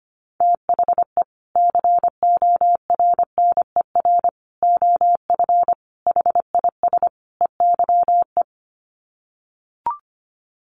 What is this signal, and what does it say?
Morse code, heard as a keyed tone, about 25 wpm, 700 Hz. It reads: THE CORNER OF HIS EYE
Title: Sets of 5 Words Encore2 25wpm